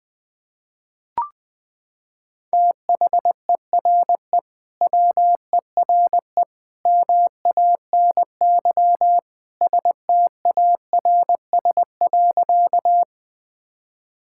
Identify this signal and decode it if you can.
Morse code, heard as a keyed tone, about 20 wpm, 700 Hz. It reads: THERE WERE MANY STARS.
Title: Sentences with 2-Word Phrases - 20wpm